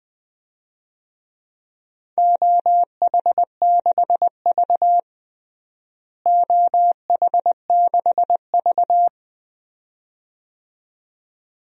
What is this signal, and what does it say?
Morse code, heard as a keyed tone, about 20 wpm, 700 Hz. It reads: OH6V OH6V
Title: DXCC-100 Call Signs - 4 Characters - Repeated 2x - 20wpm